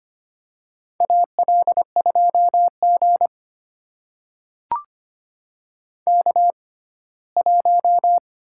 Morse code, heard as a keyed tone, about 25 wpm, spaced as for 17 wpm, 700 Hz. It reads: AL2G K1
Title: USA Call Signs - 4 Characters - 6x Character Spacing - 25wpm